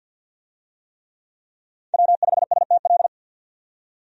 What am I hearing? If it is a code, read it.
Morse code, heard as a keyed tone, about 52 wpm, 700 Hz. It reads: W5STL
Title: USA Call Signs - 5 Characters - 50wpm